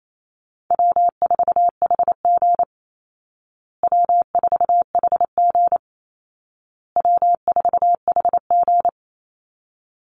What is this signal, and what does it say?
Morse code, heard as a keyed tone, about 28 wpm, 700 Hz. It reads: W4HG W4HG W4HG
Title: Call Signs Repeated 2x - 28wpm